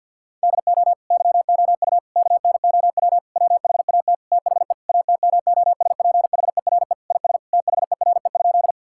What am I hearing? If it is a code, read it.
Morse code, heard as a keyed tone, about 50 wpm, 700 Hz. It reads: DO YOU KNOW WHAT THE ATMOSPHERE IS THERE?
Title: Sentences from Top 1600 Words 50wpm